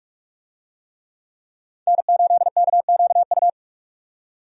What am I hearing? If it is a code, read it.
Morse code, heard as a keyed tone, about 46 wpm, 700 Hz. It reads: N8KQU
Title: USA Call Signs - 5 Characters - 45wpm